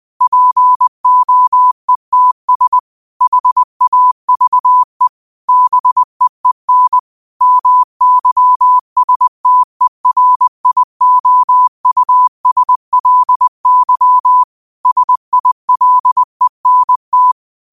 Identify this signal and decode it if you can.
Morse code, heard as a keyed tone, about 20 wpm, 1000 Hz. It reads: POETS HAVE BEEN MYSTERIOUSLY SILENT